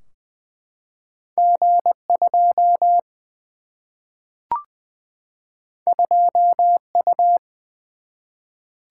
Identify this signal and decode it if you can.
Morse code, heard as a keyed tone, about 20 wpm, 700 Hz. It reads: G 2 2 U
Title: Single-Letter Single-Number 20wpm